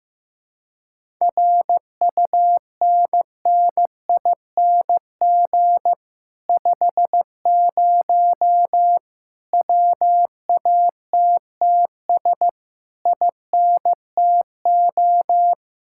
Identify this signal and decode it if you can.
Morse code, heard as a keyed tone, about 15 wpm, 700 Hz. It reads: RUNNING 50 WATTS INTO
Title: Speed-Racing Common QSO Phrases Encore - 15wpm